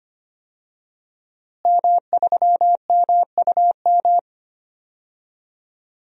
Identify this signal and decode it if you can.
Morse code, heard as a keyed tone, about 25 wpm, 700 Hz. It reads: M3MUM
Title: DXCC-100 Call Signs - 5 Characters - Repeated 2x - 25wpm